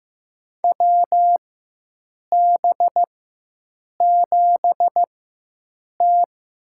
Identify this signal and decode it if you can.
Morse code, heard as a keyed tone, about 15 wpm, spaced as for 7 wpm, 700 Hz. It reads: WB7T
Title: USA Call Signs - 4x Character Spacing - 15wpm